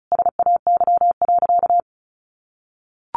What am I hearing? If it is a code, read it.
Morse code, heard as a keyed tone, about 34 wpm, 700 Hz. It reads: SAY.
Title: Sentences from Top 600 Words 35wpm